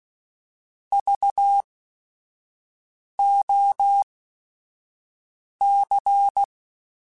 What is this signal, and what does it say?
Morse code, heard as a keyed tone, about 16 wpm, spaced as for 5 wpm, 800 Hz. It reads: VOC
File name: QSOs / QSOs-mp3s / WD0BGP 1@5